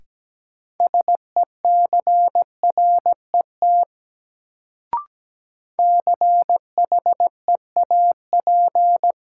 Morse code, heard as a keyed tone, about 17 wpm, 700 Hz. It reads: SECRET CHEAP
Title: Sentences from Top 1200 Words - New Word Drill - 17wpm